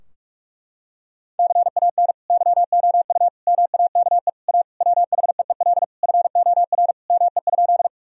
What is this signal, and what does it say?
Morse code, heard as a keyed tone, about 43 wpm, 700 Hz. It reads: CAN YOU MAKE A WHEEL FOR ME?
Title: Sentences from Top 1000 Words - Review Part 3 - 45wpm